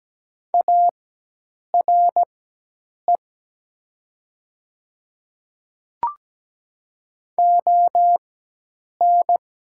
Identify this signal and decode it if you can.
Morse code, heard as a keyed tone, about 17 wpm, spaced as for 8 wpm, 690 Hz. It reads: ARE ON
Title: 4x Character Spacing - Top 300 Words - 17wpm